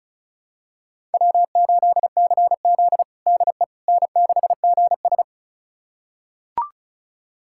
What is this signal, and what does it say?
Morse code, heard as a keyed tone, about 36 wpm, 700 Hz. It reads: W8CZ DE N6GS
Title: USA Callsign DE Callsign - 4 Characters - 35wpm